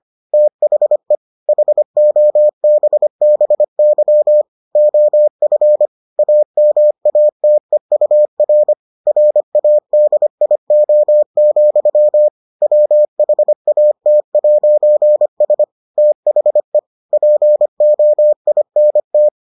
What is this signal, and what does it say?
Morse code, heard as a keyed tone, about 25 wpm, 600 Hz. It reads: THE HOBBY OF AMATEUR RADIO, WHAT'S THE POINT